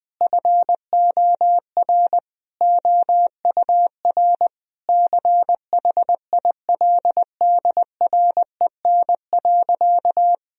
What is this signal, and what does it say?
Morse code, heard as a keyed tone, about 21 wpm, 700 Hz. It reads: FOR OUR CHILDREN.
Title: Sentences from Top 1000 Words - Review Part 4 - 20wpm